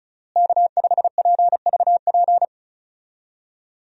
Morse code, heard as a keyed tone, about 36 wpm, 700 Hz. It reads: K5PVP
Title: USA Call Signs - 5 Characters - Repeated 3x - 35wpm